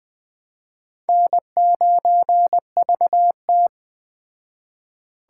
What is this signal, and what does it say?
Morse code, heard as a keyed tone, about 20 wpm, 700 Hz. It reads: N9VT
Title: Speed Racing - USA 1x2 Call Signs 17wpm